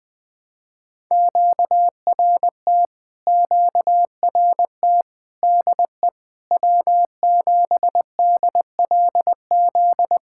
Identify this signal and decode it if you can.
Morse code, heard as a keyed tone, about 21 wpm, 700 Hz. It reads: QRT QRT DE W7DLZ
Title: POTA QSO Practice - 20wpm